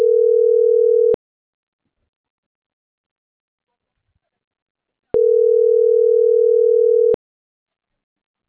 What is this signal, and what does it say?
Ringback tone (North America), 440 + 480 Hz.
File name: CENTRAL MABE / out manuel.posadas